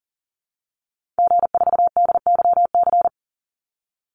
Morse code, heard as a keyed tone, about 40 wpm, 700 Hz. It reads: G4DYC
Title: Call Signs Repeated 2x - 40wpm